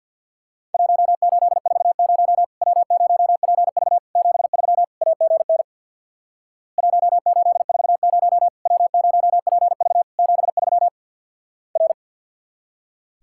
Morse code, heard as a keyed tone, about 50 wpm, 700 Hz. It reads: 1840 W0PV 73 AGN 1840 W0PV 73 R
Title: Worked All Europe DX Contest - QTC Exchange with Repeats - 50wpm